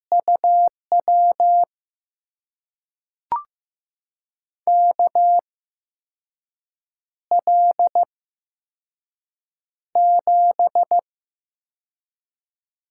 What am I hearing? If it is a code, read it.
Morse code, heard as a keyed tone, about 15 wpm, spaced as for 5 wpm, 700 Hz. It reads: UW KL7
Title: USA Call Signs - 6 Characters - 8x Character Spacing - 15wpm